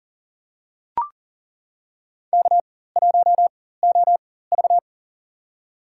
Morse code, heard as a keyed tone, about 40 wpm, spaced as for 19 wpm, 700 Hz. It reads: K1OV
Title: USA Call Signs - 4 Characters - 4x Character Spacing - 40wpm